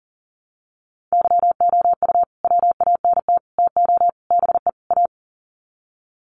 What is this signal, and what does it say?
Morse code, heard as a keyed tone, about 40 wpm, 700 Hz. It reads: YOU WANT TO BE A
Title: Sets of 5 Words Encore2 40wpm